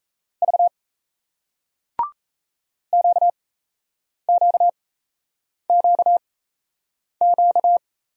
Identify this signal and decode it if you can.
Morse code, heard as a keyed tone, about 39 wpm, spaced as for 8 wpm, 700 Hz. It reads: V QQQQ